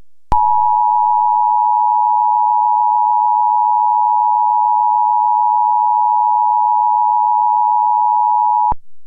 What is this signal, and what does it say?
Emergency Alert System (SAME) tones (the 853 and 960 Hz attention tone). It is heard from 0.3 to 8.8 s.
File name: National EAS Test